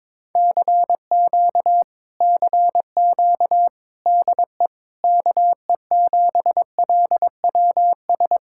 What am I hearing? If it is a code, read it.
Morse code, heard as a keyed tone, about 21 wpm, 700 Hz. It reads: CQ CQ DE KE7LWS
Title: Speed-Racing Common QSO Phrases Encore - 22wpm